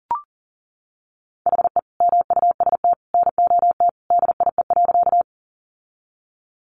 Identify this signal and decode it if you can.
Morse code, heard as a keyed tone, about 40 wpm, 700 Hz. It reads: HE MUST NOT DIE.
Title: Sentences from Top 1300 Words 40wpm